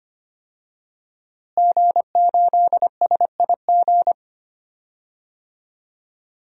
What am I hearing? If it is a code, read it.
Morse code, heard as a keyed tone, about 25 wpm, 700 Hz. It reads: G8SIG
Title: DXCC-100 Call Signs - 5 Characters - Repeated 3x - 25wpm